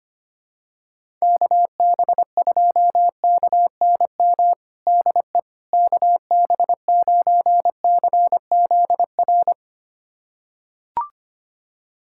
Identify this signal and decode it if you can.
Morse code, heard as a keyed tone, about 25 wpm, 700 Hz. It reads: KB2KNM DE KB9CZR
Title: USA Callsign DE Callsign - 6 Characters - 25wpm